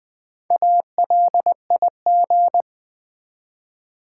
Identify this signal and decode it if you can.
Morse code, heard as a keyed tone, about 20 wpm, 700 Hz. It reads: ALIG
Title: Build A Word - 1 Letter At A Time - 5-Letter Words - 20wpm